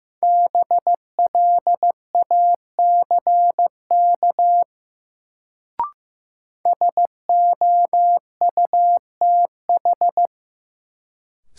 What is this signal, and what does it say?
Morse code, heard as a keyed tone, about 15 wpm, 700 Hz. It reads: BLACK SOUTH
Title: Sentences from Top 400 Words - New Word Drill - 15wpm